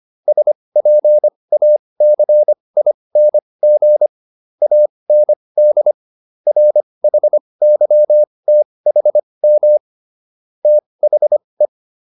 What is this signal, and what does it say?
Morse code, heard as a keyed tone, about 25 wpm, spaced as for 20 wpm, 600 Hz. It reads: SPACING AND RHYTHM THE